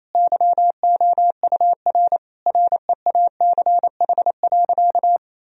Morse code, heard as a keyed tone, about 27 wpm, 700 Hz. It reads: YOUR REACH.
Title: Sentences from Top 1000 Words - Review Part 2 - 28wpm